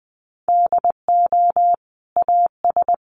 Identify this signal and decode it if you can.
Morse code, heard as a keyed tone, about 20 wpm, 700 Hz. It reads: DO AS
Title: Sentences from Top 1500 Words 20wpm